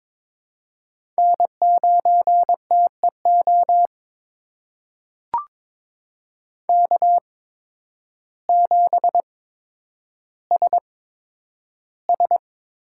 Morse code, heard as a keyed tone, about 22 wpm, spaced as for 10 wpm, 700 Hz. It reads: N9TEO K7SS
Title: USA Call Signs - 5 Characters - 8x Character Spacing - 22wpm